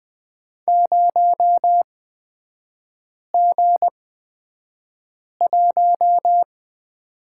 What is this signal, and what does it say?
Morse code, heard as a keyed tone, about 20 wpm, spaced as for 5 wpm, 690 Hz. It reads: 0G1
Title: 229 Lesson 37e - Letter x - Three Characters 20wpm